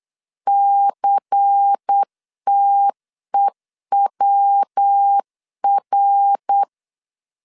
Morse code, heard as a keyed tone, about 8 wpm, 800 Hz. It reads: CTEWR
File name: LEZIONI / 6 PKX0 / 6 PKX0 8wpm 3341